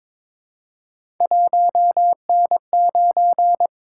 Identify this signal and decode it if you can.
Morse code, heard as a keyed tone, about 22 wpm, 700 Hz. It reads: 1N9
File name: practice / 22wpm / Three Letter-Number Combinations - 22wpm